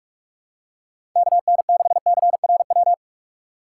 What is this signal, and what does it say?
Morse code, heard as a keyed tone, about 44 wpm, 700 Hz. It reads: KN6CRW
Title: USA Call Signs - 2x Character Spacing - 45wpm